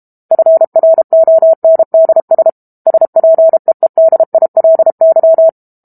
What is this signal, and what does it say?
Morse code, heard as a keyed tone, about 32 wpm, 650 Hz. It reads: FRONDS SPEEDILY
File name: Book Two / 06-Chapter 6 - 32 WPM